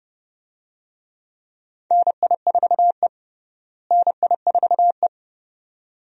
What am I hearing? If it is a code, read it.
Morse code, heard as a keyed tone, about 32 wpm, 700 Hz. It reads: NI4E NI4E
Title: USA Call Signs - 4 Characters - Repeated 2x - 30wpm